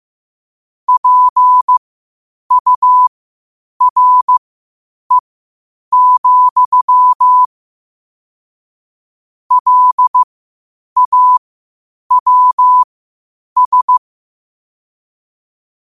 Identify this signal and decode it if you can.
Morse code, heard as a keyed tone, about 15 wpm, spaced as for 8 wpm, 1000 Hz. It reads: PURE, LAWS
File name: qotdfiles / 8wpm